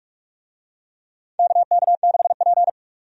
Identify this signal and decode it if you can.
Morse code, heard as a keyed tone, about 46 wpm, 700 Hz. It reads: KK6P
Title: USA Call Signs - 4 Characters - 45wpm